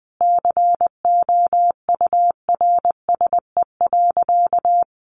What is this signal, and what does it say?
Morse code, heard as a keyed tone, about 20 wpm, 700 Hz. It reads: COURSE.
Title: Sentences from Top 1100 Words 20wpm